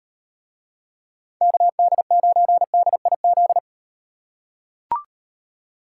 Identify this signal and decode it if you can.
Morse code, heard as a keyed tone, about 38 wpm, 700 Hz. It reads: KD9DIZ